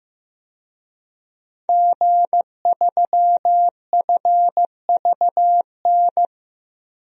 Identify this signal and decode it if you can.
Morse code, heard as a keyed tone, about 16 wpm, 700 Hz. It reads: G3FVN
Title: DXCC-100 Call Signs - 5 Characters - Repeated 3x - 15wpm